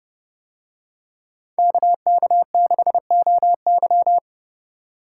Morse code, heard as a keyed tone, about 28 wpm, 700 Hz. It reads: KK6OY